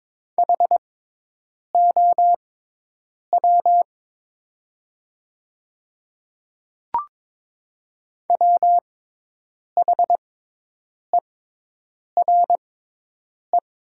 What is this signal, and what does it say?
Morse code, heard as a keyed tone, about 22 wpm, spaced as for 8 wpm, 700 Hz. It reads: HOW WHERE
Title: 6x Character Spacing - Top 300 Words - 22wpm